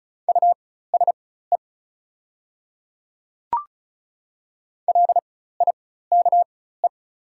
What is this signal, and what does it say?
Morse code, heard as a keyed tone, about 33 wpm, spaced as for 16 wpm, 690 Hz. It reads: USE LIKE
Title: 4x Character Spacing - Top 100 Words - 35wpm